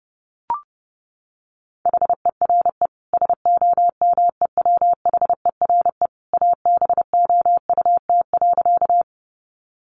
Morse code, heard as a keyed tone, about 30 wpm, 700 Hz. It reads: HERE SOMEWHERE ABOUT.
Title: Sentences from Top 1400 Words 30wpm